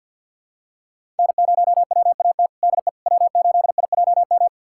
Morse code, heard as a keyed tone, about 52 wpm, 700 Hz. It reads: N0WAT DE W8IJM
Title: USA Callsign DE Callsign - 5 Characters - 50wpm